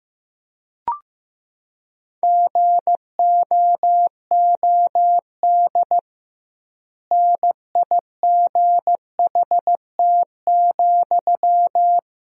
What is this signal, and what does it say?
Morse code, heard as a keyed tone, about 15 wpm, 700 Hz. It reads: GOOD NIGHT,
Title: 2x Word-Spacing - Sentences from Top 300 Words - 15wpm